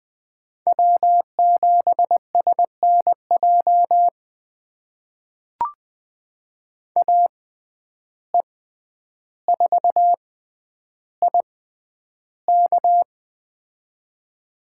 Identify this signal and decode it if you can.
Morse code, heard as a keyed tone, about 20 wpm, spaced as for 10 wpm, 700 Hz. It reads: W7SNJ AE4IK
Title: USA Call Signs - 5 Characters - 6x Character Spacing - 20wpm